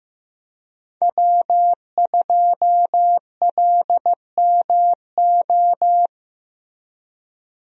Morse code, heard as a keyed tone, about 15 wpm, 700 Hz. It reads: W2LMO
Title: USA Call Signs - 5 Characters - 15wpm